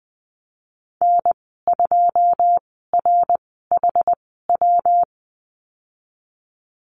Morse code, heard as a keyed tone, about 20 wpm, spaced as for 15 wpm, 700 Hz. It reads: N2RHW
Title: 128 Lesson 21g - Number 6 - Call Signs 20wpm